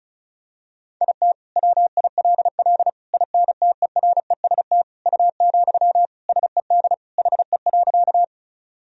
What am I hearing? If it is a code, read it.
Morse code, heard as a keyed tone, about 36 wpm, 700 Hz. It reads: IT WILL INTEREST U, SED HE.
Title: Sentences from Top 400 Words and Abbreviations 35wpm